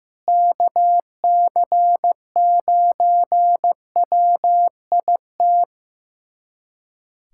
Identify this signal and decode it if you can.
Morse code, heard as a keyed tone, about 15 wpm, 700 Hz. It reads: KC9WIT
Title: USA Callsign DE Callsign 15wpm